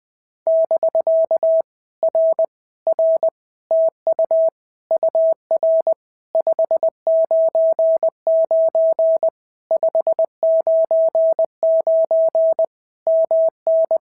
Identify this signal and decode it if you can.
Morse code, heard as a keyed tone, about 20 wpm, 650 Hz. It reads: <BK> R R TU UR 599 599 MN
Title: POTA QSO Practice - 20wpm